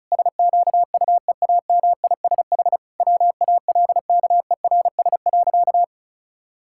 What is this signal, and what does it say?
Morse code, heard as a keyed tone, about 34 wpm, 700 Hz. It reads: SQUEAMISH WALKERS.
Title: Pangram Challenge v2 - 35wpm